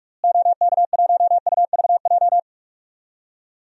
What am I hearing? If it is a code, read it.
Morse code, heard as a keyed tone, about 46 wpm, 700 Hz. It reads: OK1UVJ
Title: DXCC-100 Call Signs - 6 Characters - Repeated 2x - 45wpm